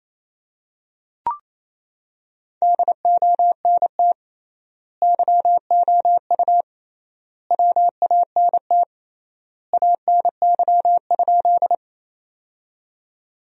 Morse code, heard as a keyed tone, about 27 wpm, 700 Hz. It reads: DONT YOU WANT ANY?
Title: 3x Word-Spacing - Sentences from Top 300 Words - 28wpm